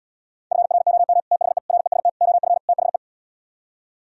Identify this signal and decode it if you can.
Morse code, heard as a keyed tone, about 77 wpm, 700 Hz. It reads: FROM THE REST OF THE